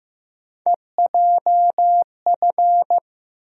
Morse code, heard as a keyed tone, about 15 wpm, 700 Hz. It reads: EJF
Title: USA Call Sign Postfixes - 3 Letters 15wpm